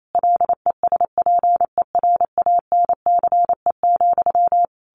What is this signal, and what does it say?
Morse code, heard as a keyed tone, about 28 wpm, 700 Hz. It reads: LESPERANCE,
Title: A Sound Of Thunder 28wpm